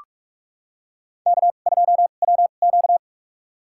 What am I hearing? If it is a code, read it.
Morse code, heard as a keyed tone, about 46 wpm, spaced as for 33 wpm, 700 Hz. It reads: K2WQ
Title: USA Call Signs - 2x Character Spacing - 45wpm